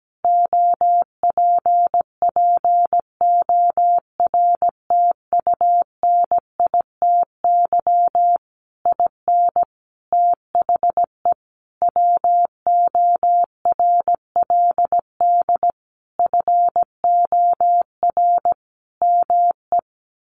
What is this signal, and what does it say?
Morse code, heard as a keyed tone, about 17 wpm, 700 Hz. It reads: OPPORTUNITY IN THE WORLD FOR ME
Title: Sentences from Top 1500 Words 17wpm